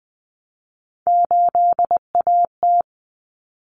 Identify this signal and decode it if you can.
Morse code, heard as a keyed tone, about 20 wpm, 700 Hz. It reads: 8AT
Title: Common Cut Numbers - 3-Digit 20wpm